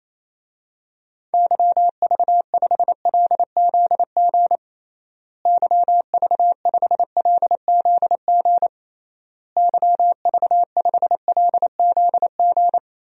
Morse code, heard as a keyed tone, about 28 wpm, 700 Hz. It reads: YV5LZG YV5LZG YV5LZG
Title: DXCC-100 Call Signs - 6 Characters - Repeated 3x - 28wpm